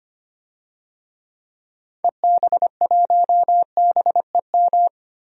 Morse code, heard as a keyed tone, about 25 wpm, 700 Hz. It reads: EB1BEM
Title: DXCC-100 Call Signs - 6 Characters - Repeated 3x - 25wpm